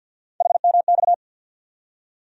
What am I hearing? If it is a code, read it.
Morse code, heard as a keyed tone, about 51 wpm, 700 Hz. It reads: HMX
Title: USA Call Sign Postfixes - 3 Letters 50wpm